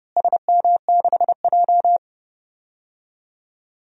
Morse code, heard as a keyed tone, about 30 wpm, 700 Hz. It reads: SM6J
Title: DXCC-100 Call Signs - 4 Characters - Repeated 3x - 30wpm